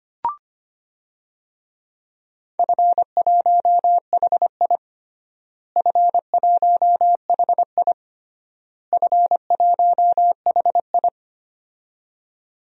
Morse code, heard as a keyed tone, about 25 wpm, 700 Hz. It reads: F1HI F1HI F1HI
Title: DXCC-100 Call Signs - 4 Characters - Repeated 3x - 25wpm